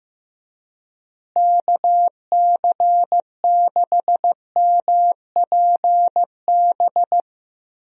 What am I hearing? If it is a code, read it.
Morse code, heard as a keyed tone, about 15 wpm, 700 Hz. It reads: KC6MPB
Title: USA Call Signs - 6 Characters - Repeated 3x - 15wpm